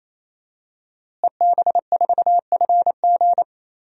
Morse code, heard as a keyed tone, about 27 wpm, 700 Hz. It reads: EB4FG
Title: DXCC-100 Call Signs - 5 Characters - Repeated 3x - 28wpm